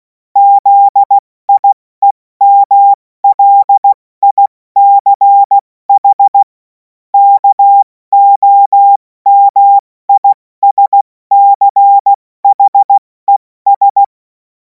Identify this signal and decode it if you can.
Morse code, heard as a keyed tone, about 16 wpm, 800 Hz. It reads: ZIEMLICH KOMISCHES
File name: Morsekurs / Witze / Witz 03. Februar 2026 70 F80